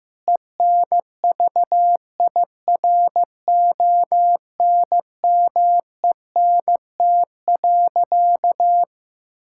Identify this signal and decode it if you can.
Morse code, heard as a keyed tone, about 15 wpm, 700 Hz. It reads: ENVIRONMENT.
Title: Sentences from Top 1000 Words - Review Part 5 - 15wpm